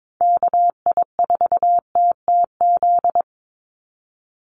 Morse code, heard as a keyed tone, about 22 wpm, 700 Hz. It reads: KI4TTZ
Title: USA Call Signs 22wpm